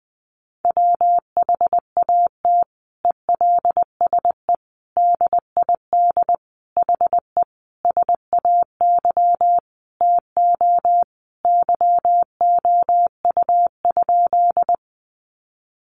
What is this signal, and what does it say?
Morse code, heard as a keyed tone, about 20 wpm, 700 Hz. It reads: WHAT ELSE DID HE SAY TO YOU?
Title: Sentences from Top 800 Words 20wpm